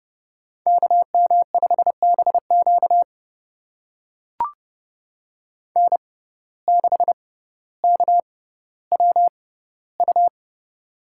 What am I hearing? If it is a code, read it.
Morse code, heard as a keyed tone, about 31 wpm, spaced as for 15 wpm, 700 Hz. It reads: KM5BQ N6KWU
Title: USA Call Signs - 5 Characters - 6x Character Spacing - 30wpm